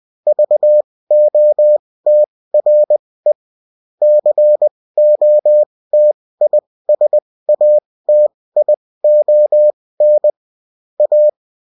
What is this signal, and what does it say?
Morse code, heard as a keyed tone, about 20 wpm, spaced as for 16 wpm, 600 Hz. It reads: VOTRE COTISATION A